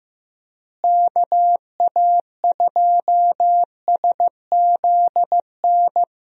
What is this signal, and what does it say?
Morse code, heard as a keyed tone, about 15 wpm, 700 Hz. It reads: KA2SZN
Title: USA Call Signs - 6 Characters - Repeated 3x - 15wpm